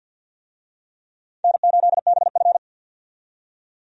Morse code, heard as a keyed tone, about 53 wpm, 700 Hz. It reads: N8BF
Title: USA Call Signs - 4 Characters - 50wpm